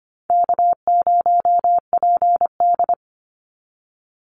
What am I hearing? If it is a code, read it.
Morse code, heard as a keyed tone, about 25 wpm, 700 Hz. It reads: K0PD
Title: Easy Call Signs - 25wpm